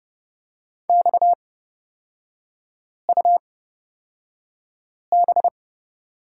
Morse code, heard as a keyed tone, about 30 wpm, spaced as for 5 wpm, 700 Hz. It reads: XUB
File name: Lesson 37e - Letter x - Three Characters 30wpm